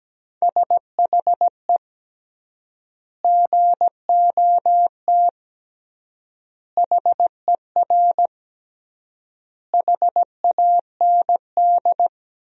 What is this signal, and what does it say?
Morse code, heard as a keyed tone, about 17 wpm, 700 Hz. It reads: SHE GOT HER HAND
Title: 3x Word-Spacing - Sentences from Top 500 Words - 17wpm